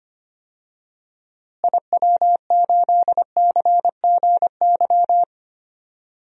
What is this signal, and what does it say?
Morse code, heard as a keyed tone, about 25 wpm, 700 Hz. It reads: IW8CGY
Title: DXCC-100 Call Signs - 6 Characters - Repeated 3x - 25wpm